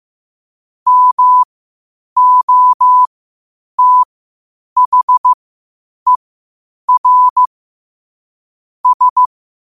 Morse code, heard as a keyed tone, about 15 wpm, spaced as for 8 wpm, 1000 Hz. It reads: MOTHER S
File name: qotdfiles / 8wpm